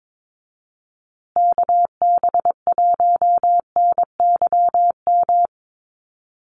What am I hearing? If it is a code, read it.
Morse code, heard as a keyed tone, about 22 wpm, 700 Hz. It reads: KB1NYM
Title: USA Call Signs Repeated 2x 22wpm